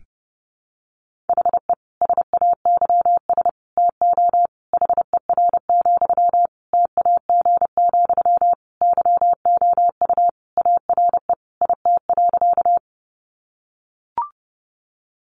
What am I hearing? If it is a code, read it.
Morse code, heard as a keyed tone, about 30 wpm, 700 Hz. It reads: HE SAYS TO HER, TAG, YOU ARE IT.
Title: Additional Easy Phrases 30wpm